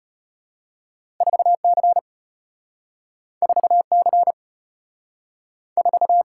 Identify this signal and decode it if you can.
Morse code, heard as a keyed tone, about 32 wpm, 700 Hz. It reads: 4 C 4 C 4